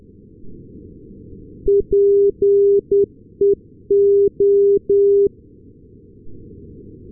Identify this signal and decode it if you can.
Morse code, heard as a keyed tone, about 10 wpm, 400 Hz. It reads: PEO
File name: europe / s-280 ndb nykoping-peola peo-398, sm1otx